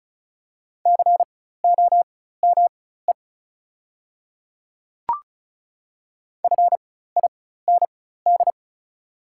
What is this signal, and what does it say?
Morse code, heard as a keyed tone, about 36 wpm, spaced as for 16 wpm, 700 Hz. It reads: COME FIND
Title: 4x Character Spacing - Top 200 Words - 35wpm